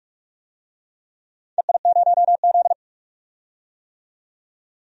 Morse code, heard as a keyed tone, about 43 wpm, 700 Hz. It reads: EI0Z
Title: DXCC-100 Call Signs - 4 Characters - 45wpm